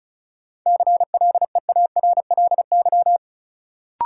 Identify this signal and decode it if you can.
Morse code, heard as a keyed tone, about 34 wpm, 700 Hz. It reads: CLEARLY
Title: Sentences from Top 1300 Words - New Word Drill Alt - 35wpm